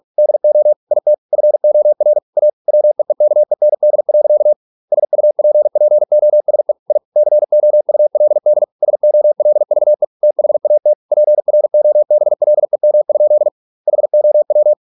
Morse code, heard as a keyed tone, about 46 wpm, 600 Hz. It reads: DO IT FOR A WEEKEND. SUPPOSE I COULD SOLVE THAT PROBLEM? HOW